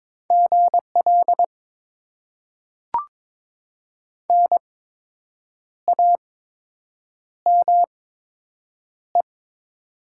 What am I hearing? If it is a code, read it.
Morse code, heard as a keyed tone, about 22 wpm, spaced as for 7 wpm, 700 Hz. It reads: GL NAME
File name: Top 90 QSO Elements - 8x Character Spacing - 22wpm